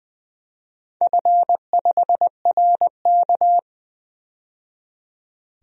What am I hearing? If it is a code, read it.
Morse code, heard as a keyed tone, about 20 wpm, 700 Hz. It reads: F5RK
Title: DXCC-100 Call Signs - 4 Characters - 20wpm